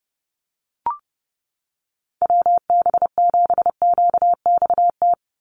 Morse code, heard as a keyed tone, about 30 wpm, 700 Hz. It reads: WB7QXT
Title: USA Call Signs 30wpm